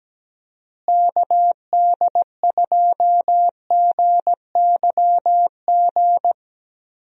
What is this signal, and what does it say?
Morse code, heard as a keyed tone, about 17 wpm, 700 Hz. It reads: KD2GYG
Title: USA Call Signs - 6 Characters - Repeated 2x - 17wpm